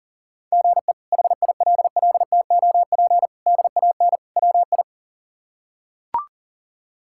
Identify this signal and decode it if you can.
Morse code, heard as a keyed tone, about 37 wpm, 700 Hz. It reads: GE HILLTOP DAN WI
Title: K1USN SST QSO Practice - 40wpm